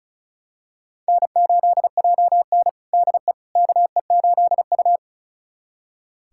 Morse code, heard as a keyed tone, about 36 wpm, 700 Hz. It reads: N8JN DE KE8U
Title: USA Callsign DE Callsign - 4 Characters - 35wpm